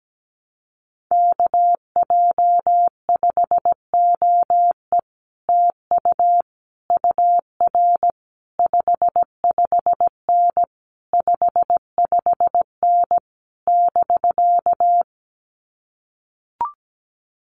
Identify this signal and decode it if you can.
Morse code, heard as a keyed tone, about 17 wpm, 700 Hz. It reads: KJ5OE TU UR 55N 55N <BK>
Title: Common QSO Phrases Encore 17wpm